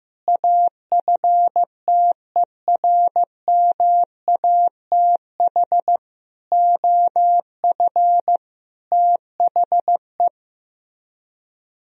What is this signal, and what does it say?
Morse code, heard as a keyed tone, about 15 wpm, 700 Hz. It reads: AFTERMATH OF THE
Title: Speed-Racing - Sets of 5 Words 15wpm